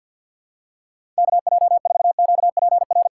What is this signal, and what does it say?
Morse code, heard as a keyed tone, about 48 wpm, 700 Hz. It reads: KJ4QPR
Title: USA Callsign DE Callsign 50wpm